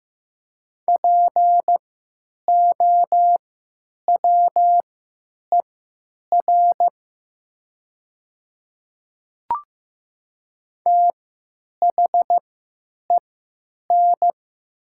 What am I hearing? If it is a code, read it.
Morse code, heard as a keyed tone, about 15 wpm, spaced as for 9 wpm, 700 Hz. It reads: POWER THEN
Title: 3x Character Spacing - Top 500 Words - 15wpm